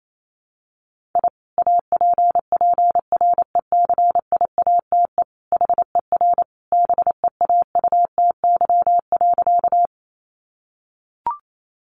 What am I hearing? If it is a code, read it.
Morse code, heard as a keyed tone, about 28 wpm, 700 Hz. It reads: I APPRECIATE HER BEAUTY.
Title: Sentences from Top 1300 Words 28wpm